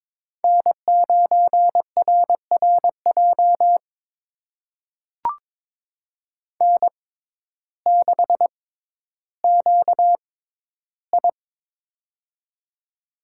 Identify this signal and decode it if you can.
Morse code, heard as a keyed tone, about 21 wpm, spaced as for 12 wpm, 700 Hz. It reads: N9RRJ N6QI
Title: USA Call Signs - 6x Character Spacing - 22wpm